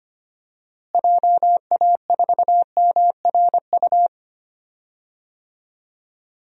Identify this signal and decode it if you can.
Morse code, heard as a keyed tone, about 25 wpm, 700 Hz. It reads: JA4MRU
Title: DXCC-100 Call Signs - 6 Characters - Repeated 2x - 25wpm